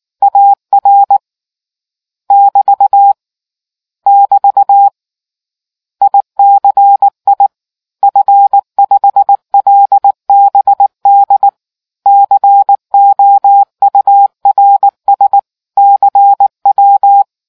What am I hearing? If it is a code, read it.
Morse code, heard as a keyed tone, about 19 wpm, 800 Hz. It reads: AR <BT> <BT> ICI F5LBD COURS CW